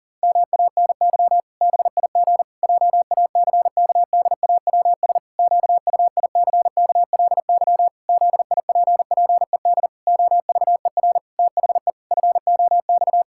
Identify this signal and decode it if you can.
Morse code, heard as a keyed tone, about 40 wpm, 700 Hz. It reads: MANY BIG JACKDAWS QUICKLY ZIPPED OVER THE FOX
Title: Pangram Challenge v2 - 40wpm